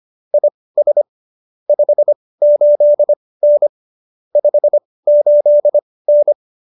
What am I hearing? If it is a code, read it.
Morse code, heard as a keyed tone, about 25 wpm, spaced as for 18 wpm, 590 Hz. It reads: IS 58N 58N
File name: qso222 18